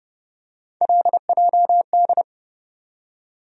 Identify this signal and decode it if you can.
Morse code, heard as a keyed tone, about 30 wpm, 700 Hz. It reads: LJD
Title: USA Call Sign Postfixes - 3 Letters 30wpm